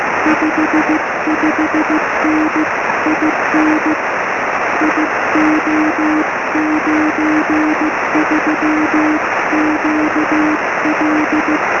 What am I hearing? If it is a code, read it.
Morse code, heard as a keyed tone, about 15 wpm, 310 Hz, with noise. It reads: 55N IN IO93QL
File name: sm7lcb / g4cch 23cm